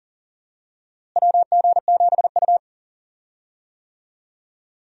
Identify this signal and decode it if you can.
Morse code, heard as a keyed tone, about 38 wpm, 700 Hz. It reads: WG7U
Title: USA Call Signs - 4 Characters - Repeated 3x - 40wpm